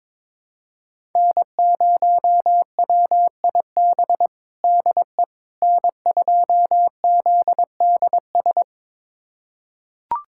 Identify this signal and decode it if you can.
Morse code, heard as a keyed tone, about 22 wpm, 700 Hz. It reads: N0WIB DE N2ZDS
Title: USA Callsign DE Callsign - 5 Characters - 22wpm